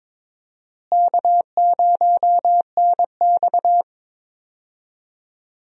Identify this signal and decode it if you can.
Morse code, heard as a keyed tone, about 21 wpm, 700 Hz. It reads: K0NX
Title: USA Call Signs - 4 Characters - Repeated 2x - 22wpm